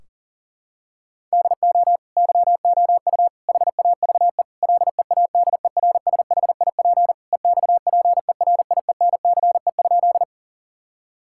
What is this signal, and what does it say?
Morse code, heard as a keyed tone, about 40 wpm, 700 Hz. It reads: DO YOU HAVE LEADERSHIP EXPERIENCE?
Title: Sentences from Top 1600 Words 40wpm